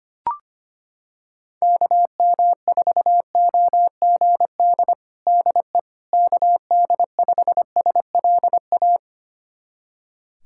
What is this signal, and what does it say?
Morse code, heard as a keyed tone, about 25 wpm, 700 Hz. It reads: KM4OGD DE KD5SLA
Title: USA Callsign DE Callsign - 6 Characters - 25wpm